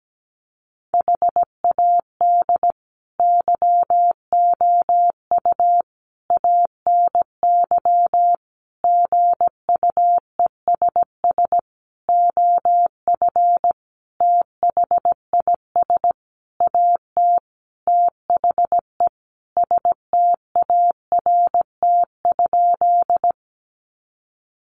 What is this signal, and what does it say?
Morse code, heard as a keyed tone, about 17 wpm, 700 Hz. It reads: HAD YOU ANY GUESS OF THIS AT THE START?
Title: Sentences from Top 1500 Words 17wpm